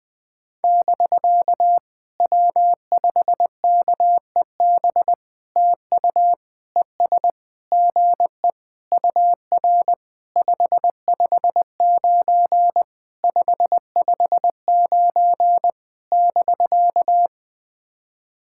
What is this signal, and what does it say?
Morse code, heard as a keyed tone, about 20 wpm, 700 Hz. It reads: <BK> W5KEB TU ES GE UR 559 559 <BK>
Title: POTA QSO Practice - 20wpm